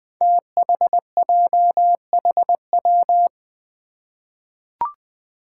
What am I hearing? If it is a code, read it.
Morse code, heard as a keyed tone, about 20 wpm, 700 Hz. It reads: THJHW
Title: Five Challenging Letter-Number Combinations 20wpm